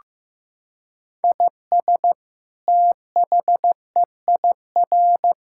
Morse code, heard as a keyed tone, about 16 wpm, 700 Hz. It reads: IS THEIR
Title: Sentences from Top 1000 Words - Review Part 5 - 15wpm